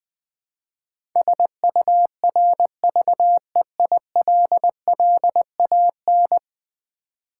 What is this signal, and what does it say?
Morse code, heard as a keyed tone, about 20 wpm, 700 Hz. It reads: SURVEILLAN
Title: Build A Word - 5 Letters At A Time - 12-Letter Words - 20wpm